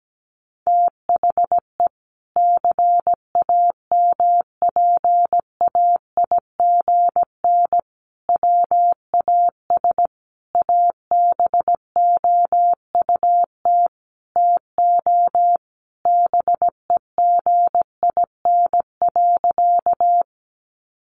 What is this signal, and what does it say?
Morse code, heard as a keyed tone, about 17 wpm, 700 Hz. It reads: THE CAMPAIGN WAS ABOUT TO BEGIN.
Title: Sentences from Top 1200 Words 17wpm